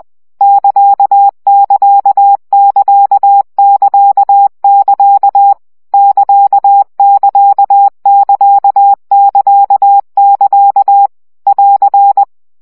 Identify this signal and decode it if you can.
Morse code, heard as a keyed tone, about 20 wpm, 800 Hz. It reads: <KA><KA><KA><KA><KA> <KA><KA><KA><KA><KA> <AR>
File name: CW gyakorlas ct 17 baud